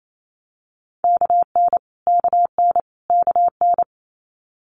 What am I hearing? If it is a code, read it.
Morse code, heard as a keyed tone, about 28 wpm, 700 Hz. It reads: KN KN KN
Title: Multi-letter Phonemes Repeated 3x 28wpm